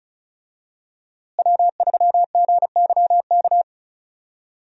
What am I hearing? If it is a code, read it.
Morse code, heard as a keyed tone, about 36 wpm, 700 Hz. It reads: W3GYK
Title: USA Call Signs - 5 Characters - Repeated 3x - 35wpm